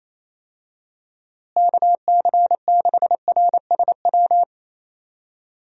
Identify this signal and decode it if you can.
Morse code, heard as a keyed tone, about 28 wpm, 700 Hz. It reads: KC6RSW
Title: USA Call Signs - 6 Characters - Repeated 3x - 28wpm